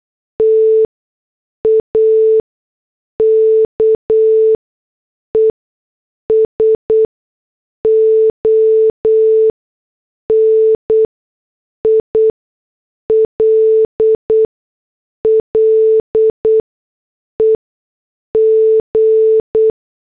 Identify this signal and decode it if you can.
Morse code, heard as a keyed tone, about 8 wpm, spaced as for 6 wpm, 440 Hz. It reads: TAKES ON ILLEG